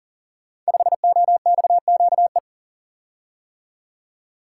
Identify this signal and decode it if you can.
Morse code, heard as a keyed tone, about 40 wpm, 700 Hz. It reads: 5OXQE
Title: Code Groups - 5 Letter:Numbers - 40wpm